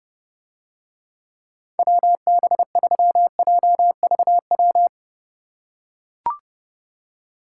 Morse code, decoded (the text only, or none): WB3JVW